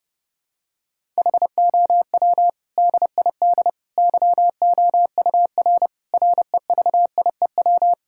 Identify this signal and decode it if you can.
Morse code, heard as a keyed tone, about 32 wpm, 700 Hz. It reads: HOW DID YOUR REVIEW
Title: Sentences from Top 1000 Words - Review Part 6 - 30wpm